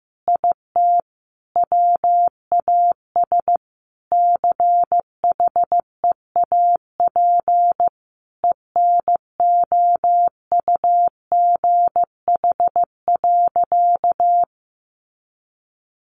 Morse code, heard as a keyed tone, about 14 wpm, 700 Hz. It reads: IT WAS CHEAP ENOUGH.
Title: Sentences from Top 1200 Words 15wpm